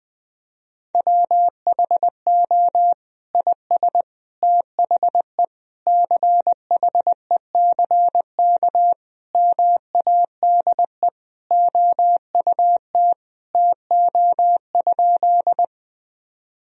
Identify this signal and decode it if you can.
Morse code, heard as a keyed tone, about 21 wpm, 700 Hz. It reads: WHO IS THE CHECK MADE OUT TO?
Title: Sentences from Top 1000 Words - Review Part 3 - 20wpm